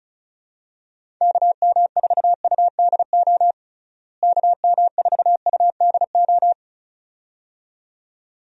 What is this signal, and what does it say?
Morse code, heard as a keyed tone, about 34 wpm, 700 Hz. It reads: KM4UDO KM4UDO
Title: USA Call Signs - 6 Characters - Repeated 2x - 35wpm